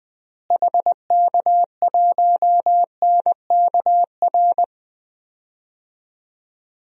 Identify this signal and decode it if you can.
Morse code, heard as a keyed tone, about 20 wpm, 690 Hz. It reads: HK1NKR
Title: DXCC-100 Call Signs - 6 Characters - Repeated 2x - 20wpm